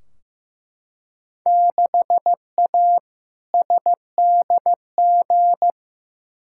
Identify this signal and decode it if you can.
Morse code, heard as a keyed tone, about 15 wpm, 700 Hz. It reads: 6A SDG
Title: Field Day Exchanges Encore - 15wpm